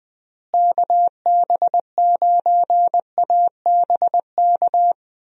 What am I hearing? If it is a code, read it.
Morse code, heard as a keyed tone, about 20 wpm, 700 Hz. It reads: KB9ABK
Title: USA Call Signs - 6 Characters - Repeated 3x - 20wpm